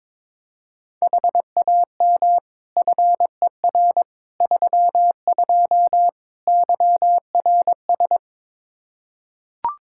Morse code, heard as a keyed tone, about 22 wpm, 700 Hz. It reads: HAM FER 32 YRS
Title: Practice QSO - Exchange 3 - More Info - 22wpm